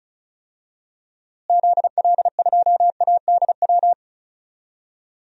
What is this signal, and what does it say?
Morse code, heard as a keyed tone, about 36 wpm, 700 Hz. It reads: ZL2ADW
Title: DXCC-100 Call Signs - 6 Characters - Repeated 3x - 35wpm